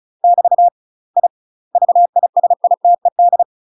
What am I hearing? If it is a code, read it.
Morse code, heard as a keyed tone, about 34 wpm, 700 Hz. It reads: <BT> I VISITED
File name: UK MASTER QSOs TRAINING / 35WPM / UK MASTER QSOs 0003